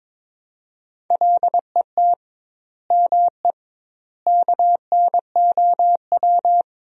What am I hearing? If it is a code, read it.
Morse code, heard as a keyed tone, about 22 wpm, 700 Hz. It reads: LET ME KNOW
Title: 2x Word-Spacing - Sentences from Top 300 Words - 22wpm